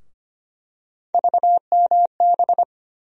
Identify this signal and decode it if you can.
Morse code, heard as a keyed tone, about 25 wpm, 700 Hz. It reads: VMB
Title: USA Call Sign Postfixes - 3 Letters 25wpm